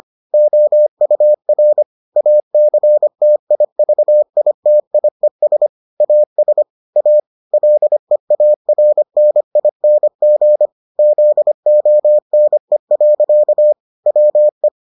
Morse code, heard as a keyed tone, about 25 wpm, 600 Hz. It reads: OUR ACTIVITIES AS A LEARNING ZONE. WE